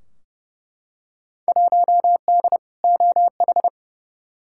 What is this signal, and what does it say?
Morse code, heard as a keyed tone, about 30 wpm, 700 Hz. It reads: 1D OH
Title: Field Day Exchanges - 30wpm